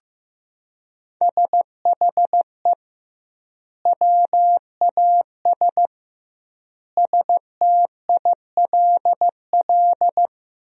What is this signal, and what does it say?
Morse code, heard as a keyed tone, about 14 wpm, 700 Hz. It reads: SHE WAS STILL
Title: 2x Word-Spacing - Sentences from Top 500 Words - 15wpm